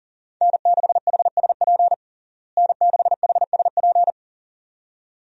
Morse code, heard as a keyed tone, about 40 wpm, 700 Hz. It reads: N6HSP N6HSP
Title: USA Call Signs - 5 Characters - Repeated 2x - 40wpm